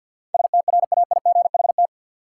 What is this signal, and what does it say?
Morse code, heard as a keyed tone, about 52 wpm, 700 Hz. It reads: STRAIGHT
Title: Sentences from Top 1000 Words - New Word Drill - 50wpm